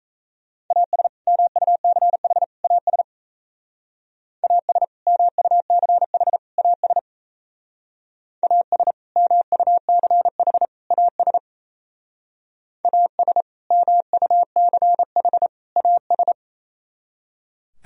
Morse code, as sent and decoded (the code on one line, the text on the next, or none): .- ... / -- ..- -.-. .... / .- ... / .- ... / -- ..- -.-. .... / .- ... / .- ... / -- ..- -.-. .... / .- ... / .- ... / -- ..- -.-. .... / .- ...
AS MUCH AS AS MUCH AS AS MUCH AS AS MUCH AS